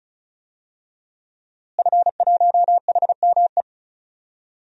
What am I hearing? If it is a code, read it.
Morse code, heard as a keyed tone, about 35 wpm, 700 Hz. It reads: F1HME
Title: DXCC-100 Call Signs - 5 Characters - Repeated 3x - 35wpm